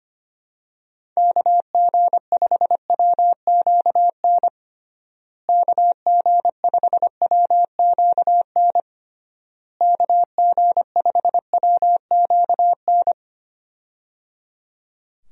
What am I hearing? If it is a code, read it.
Morse code, heard as a keyed tone, about 25 wpm, 700 Hz. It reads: KG5WQN KG5WQN KG5WQN
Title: USA Call Signs - 6 Characters - Repeated 3x - 25wpm